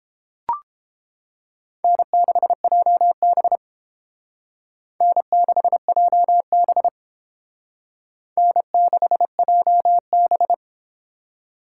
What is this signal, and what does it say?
Morse code, heard as a keyed tone, about 30 wpm, 700 Hz. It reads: N6JB N6JB N6JB